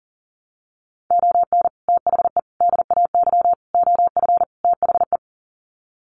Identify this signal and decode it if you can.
Morse code, heard as a keyed tone, about 40 wpm, 700 Hz. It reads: ON THE DAY OF THE
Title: Sets of 5 Words Encore 40wpm